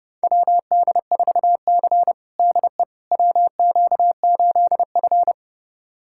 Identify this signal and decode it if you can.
Morse code, heard as a keyed tone, about 30 wpm, 700 Hz. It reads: WD4C DE WQ8F
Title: USA Callsign DE Callsign - 4 Characters - 30wpm